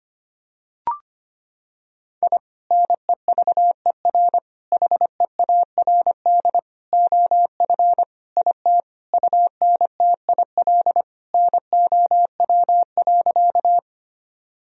Morse code, heard as a keyed tone, about 25 wpm, 700 Hz. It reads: I NEVER HEARD OF IT UNTIL NOW.
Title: Sentences from Top 1000 Words - Review Part 1 - 25wpm